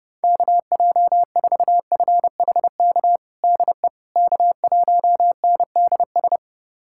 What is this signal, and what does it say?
Morse code, heard as a keyed tone, about 30 wpm, 690 Hz. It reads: KJ4FHK DE K1NDS
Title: USA Callsign DE Callsign 30wpm